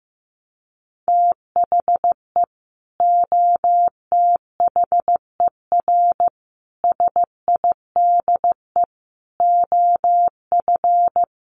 Morse code, heard as a keyed tone, about 14 wpm, 700 Hz. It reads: THE OTHER SIDE OF
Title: Sets of 5 Words 15wpm